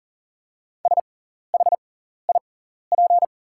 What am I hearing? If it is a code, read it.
Morse code, heard as a keyed tone, about 40 wpm, spaced as for 14 wpm, 700 Hz. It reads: SHIP
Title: 6x Character Spacing - Top 500 Words - 40wpm